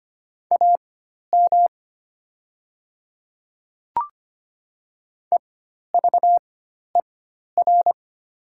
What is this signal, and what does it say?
Morse code, heard as a keyed tone, about 25 wpm, spaced as for 12 wpm, 700 Hz. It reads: AM EVER
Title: 4x Character Spacing - Top 500 Words - 25wpm